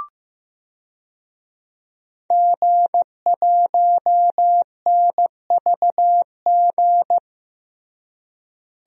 Morse code, heard as a keyed tone, about 15 wpm, 700 Hz. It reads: G1NVG
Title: DXCC-100 Call Signs - 5 Characters - Repeated 2x - 15wpm